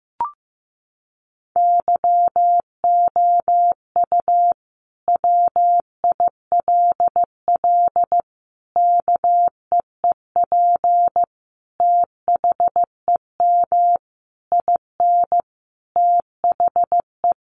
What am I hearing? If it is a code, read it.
Morse code, heard as a keyed tone, about 15 wpm, 700 Hz. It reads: YOU WILL KEEP THEM IN THE
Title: Sentences from Top 1300 Words 15wpm